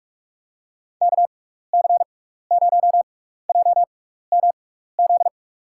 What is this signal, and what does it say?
Morse code, heard as a keyed tone, about 45 wpm, spaced as for 15 wpm, 700 Hz. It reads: KC0JMZ
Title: USA Call Signs - 6 Characters - 6x Character Spacing - 45wpm